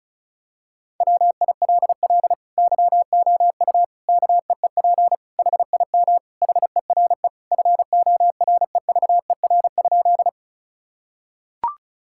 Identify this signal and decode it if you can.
Morse code, heard as a keyed tone, about 35 wpm, 700 Hz. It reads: WILL YOU KEEP HIM HERE FOREVER?
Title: Sentences from Top 1600 Words 35wpm